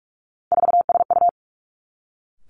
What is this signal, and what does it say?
Morse code, heard as a keyed tone, about 46 wpm, 700 Hz. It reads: 4SU
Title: Three Challenging Letter-Number Combinations 45wpm